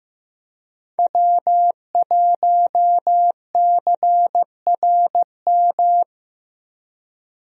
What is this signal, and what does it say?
Morse code, heard as a keyed tone, about 15 wpm, 700 Hz. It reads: W1CRM
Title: USA Call Signs - 5 Characters - 15wpm